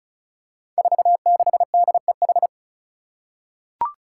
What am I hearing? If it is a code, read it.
Morse code, heard as a keyed tone, about 36 wpm, 700 Hz. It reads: 46DEH
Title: Five Challenging Letter-Number Combinations 35wpm